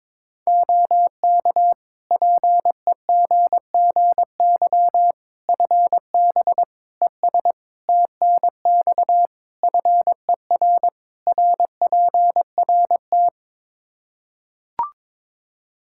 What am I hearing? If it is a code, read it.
Morse code, heard as a keyed tone, about 22 wpm, 700 Hz. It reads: OK PEGGY FB ES TNX FER RPRT
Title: Practice QSO - Exchange 2 - Info - 22wpm